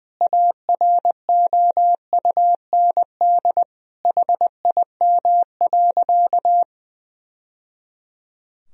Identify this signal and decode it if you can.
Morse code, heard as a keyed tone, about 20 wpm, 700 Hz. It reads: AROUND HIM.
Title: Sentences from Top 1000 Words - Review Part 3 - 20wpm